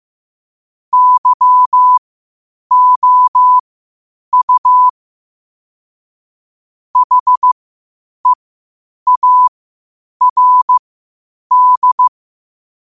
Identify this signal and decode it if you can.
Morse code, heard as a keyed tone, about 15 wpm, spaced as for 8 wpm, 1000 Hz. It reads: YOU HEARD
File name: qotdfiles / 8wpm